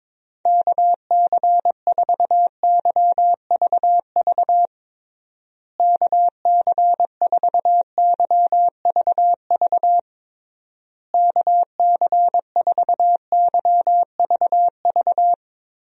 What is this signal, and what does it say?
Morse code, heard as a keyed tone, about 22 wpm, 700 Hz. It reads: KC4YVV KC4YVV KC4YVV
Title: USA Call Signs - 6 Characters - Repeated 3x - 22wpm